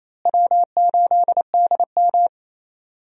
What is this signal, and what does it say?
Morse code, heard as a keyed tone, about 28 wpm, 700 Hz. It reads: W8DM
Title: USA Call Signs - 4 Characters - Repeated 3x - 28wpm